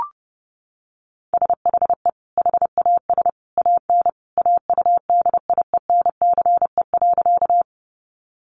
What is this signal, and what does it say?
Morse code, heard as a keyed tone, about 30 wpm, 700 Hz. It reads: SHE HAS AN AUDIENCE.
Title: Sentences from Top 800 Words 30wpm